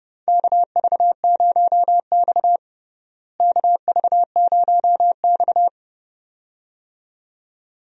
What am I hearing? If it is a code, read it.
Morse code, heard as a keyed tone, about 32 wpm, 700 Hz. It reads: KV0X KV0X
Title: USA Call Signs - 4 Characters - Repeated 2x - 30wpm